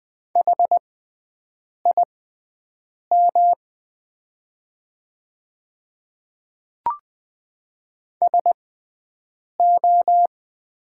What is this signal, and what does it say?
Morse code, heard as a keyed tone, about 20 wpm, spaced as for 7 wpm, 700 Hz. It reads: HIM SO
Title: 6x Character Spacing - Top 100 Words - 20wpm